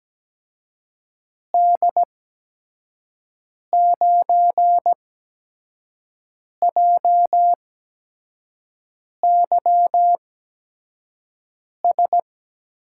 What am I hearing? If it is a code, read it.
Morse code, heard as a keyed tone, about 17 wpm, 700 Hz. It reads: D 9 J Y S
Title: USA Call Signs - 6 Characters - 8x Character Spacing - 17wpm